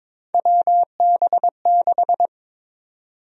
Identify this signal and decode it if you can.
Morse code, heard as a keyed tone, about 22 wpm, 700 Hz. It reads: WB6
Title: USA Call Sign Prefixes - 22wpm